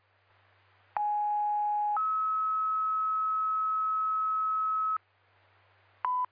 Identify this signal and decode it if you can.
Two-tone page, heard as a one-second tone, then a three-second tone, about 830 then 1290 Hz, with noise. It is heard from 1.0 to 5.0 s.